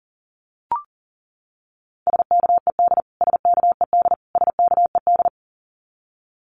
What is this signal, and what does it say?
Morse code, heard as a keyed tone, about 40 wpm, 700 Hz. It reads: SKED SKED SKED
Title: Abbreviations Repeated 3x - 40wpm